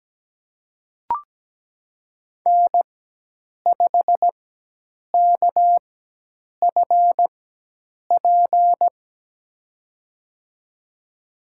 Morse code, heard as a keyed tone, about 17 wpm, spaced as for 8 wpm, 700 Hz. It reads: N5KFP
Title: USA Call Signs - 5 Characters - 4x Character Spacing - 17wpm